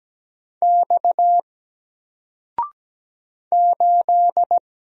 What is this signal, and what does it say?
Morse code, heard as a keyed tone, about 17 wpm, 700 Hz. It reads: X 8
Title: Single Character (Letter:Number:Punctuation:Prosign) 17wpm